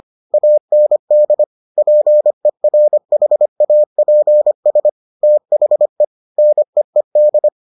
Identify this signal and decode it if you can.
Morse code, heard as a keyed tone, about 25 wpm, 600 Hz. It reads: AND PERHAPS THE NEED